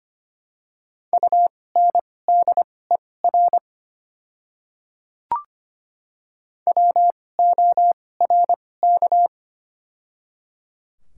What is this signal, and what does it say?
Morse code, heard as a keyed tone, about 25 wpm, spaced as for 18 wpm, 690 Hz. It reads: UNDER WORK
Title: 2x Character Spacing - Top 200 Words - 25wpm